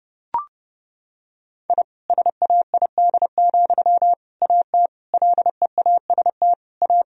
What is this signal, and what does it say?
Morse code, heard as a keyed tone, about 29 wpm, 700 Hz. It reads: I SAID, AT LEAST A
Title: Sentences from Top 1000 Words - Review Part 2 - 30wpm